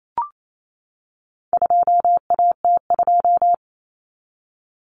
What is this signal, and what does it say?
Morse code, heard as a keyed tone, about 28 wpm, 700 Hz. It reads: 2AT2
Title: Common Cut Numbers - 4-Digit 28wpm